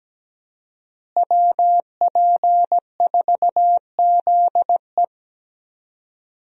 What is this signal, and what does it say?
Morse code, heard as a keyed tone, about 17 wpm, 700 Hz. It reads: WP4ZE
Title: USA Call Signs - 5 Characters - Repeated 3x - 17wpm